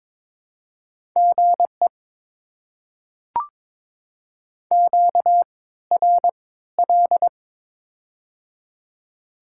Morse code, heard as a keyed tone, about 22 wpm, spaced as for 15 wpm, 700 Hz. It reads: GE QRL
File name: Top 20 QSO Elements - 3x Character Spacing - 22wpm